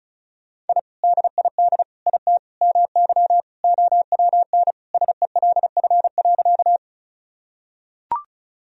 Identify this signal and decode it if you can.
Morse code, heard as a keyed tone, about 36 wpm, 700 Hz. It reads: I DID IT MY OWN SELF.
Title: Sentences from Top 1000 Words - Review Part 1 - 35wpm